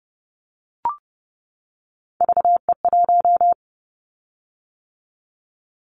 Morse code, heard as a keyed tone, about 30 wpm, 700 Hz. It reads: VE1
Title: Callsign Prefixes for Top 100 DXCC - 30wpm